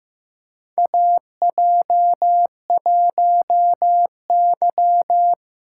Morse code, heard as a keyed tone, about 15 wpm, 700 Hz. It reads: AJ1Y
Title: USA Call Signs - 4 Characters - 2x Character Spacing - 15wpm